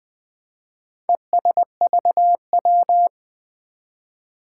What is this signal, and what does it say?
Morse code, heard as a keyed tone, about 20 wpm, 700 Hz. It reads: ESVW
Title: Four Challenging Letter-Number Combinations 20wpm